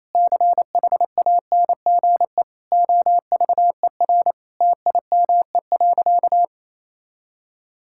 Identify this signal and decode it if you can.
Morse code, heard as a keyed tone, about 28 wpm, 700 Hz. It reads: CHANGE OVER TIME.
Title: Sentences from Top 1000 Words - Review Part 6 - 28wpm